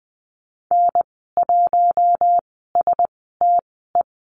Morse code, heard as a keyed tone, about 20 wpm, spaced as for 14 wpm, 700 Hz. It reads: N1STE
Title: 042 Lesson 8g - Number 1 - Call Signs 20wpm